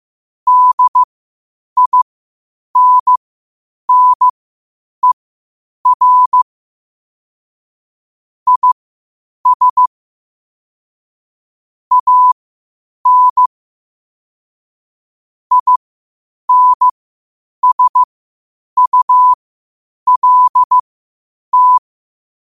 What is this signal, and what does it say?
Morse code, heard as a keyed tone, about 15 wpm, spaced as for 8 wpm, 1000 Hz. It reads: DINNER IS AN INSULT